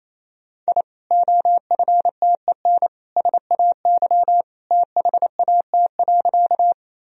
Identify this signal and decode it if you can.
Morse code, heard as a keyed tone, about 28 wpm, 700 Hz. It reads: I OFTEN SAY THAT.
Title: Sentences from Top 1000 Words - Review Part 1 - 28wpm